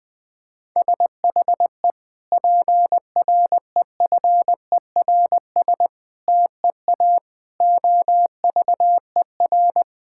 Morse code, heard as a keyed tone, about 20 wpm, 700 Hz. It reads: SHE PREFERS TEA OVER
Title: Difficult Sentences with Hints - 20wpm